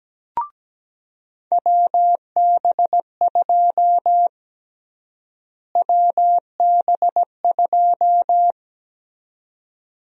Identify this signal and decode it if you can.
Morse code, heard as a keyed tone, about 17 wpm, 700 Hz. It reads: WB2 WB2
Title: USA Call Sign Prefixes - Repeated 2x - 17wpm